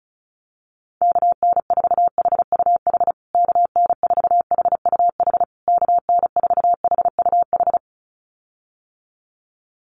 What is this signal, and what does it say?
Morse code, heard as a keyed tone, about 34 wpm, 700 Hz. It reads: KN4HUH KN4HUH KN4HUH
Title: USA Call Signs Repeated 3x 35wpm